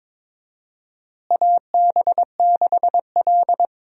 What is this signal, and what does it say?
Morse code, heard as a keyed tone, about 21 wpm, 700 Hz. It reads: AB6L
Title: USA Call Signs - 4 Characters - Repeated 3x - 22wpm